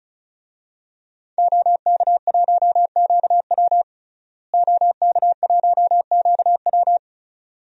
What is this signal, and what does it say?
Morse code, heard as a keyed tone, about 36 wpm, 700 Hz. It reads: OK1QW OK1QW
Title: DXCC-100 Call Signs - 5 Characters - Repeated 2x - 35wpm